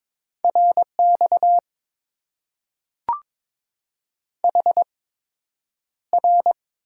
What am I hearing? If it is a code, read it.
Morse code, heard as a keyed tone, about 22 wpm, spaced as for 9 wpm, 700 Hz. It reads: RX HR
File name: Top 90 QSO Elements - 8x Character Spacing - 22wpm